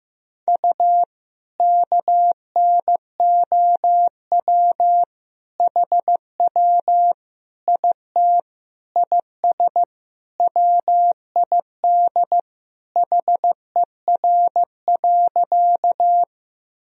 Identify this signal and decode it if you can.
Morse code, heard as a keyed tone, about 15 wpm, 700 Hz. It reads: U KNOW HW IT IS WID HER.
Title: Sentences from Top 100 Words and Abbreviations 15wpm